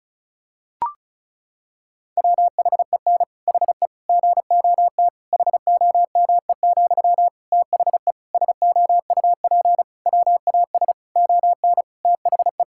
Morse code, heard as a keyed tone, about 34 wpm, 700 Hz. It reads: WHEN HE GOT HOME, THE SOUP WAS ON THE
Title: Sentences from Top 1000 Words - Review Part 6 - 35wpm